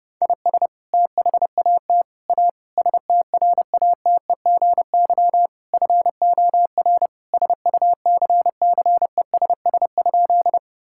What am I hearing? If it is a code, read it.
Morse code, heard as a keyed tone, about 30 wpm, 700 Hz. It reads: IS THAT A STRATEGY FOR SUCCESS?
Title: Sentences from Top 1000 Words - Review Part 5 - 30wpm